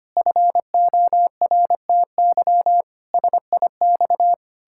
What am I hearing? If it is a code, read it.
Morse code, heard as a keyed tone, about 25 wpm, 700 Hz. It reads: FORTY SIX
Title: Pangram Challenge v2 - 25wpm